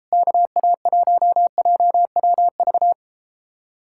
Morse code, heard as a keyed tone, about 33 wpm, 700 Hz. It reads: KA1JWV